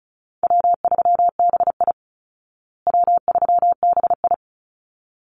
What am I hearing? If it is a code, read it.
Morse code, heard as a keyed tone, about 34 wpm, 700 Hz. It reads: W3BI W3BI
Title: Easy Call Signs Repeated 2x - 35wpm